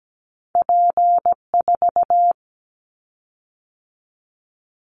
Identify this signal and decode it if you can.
Morse code, heard as a keyed tone, about 17 wpm, 700 Hz. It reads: P4
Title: Callsign Prefixes for Top 100 DXCC - 17wpm